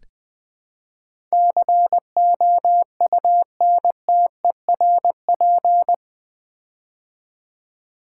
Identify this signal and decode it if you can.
Morse code, heard as a keyed tone, about 20 wpm, 700 Hz. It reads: COUNTERP
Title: Build A Word - 4 Letters At A Time - 11-Letter Words - 20wpm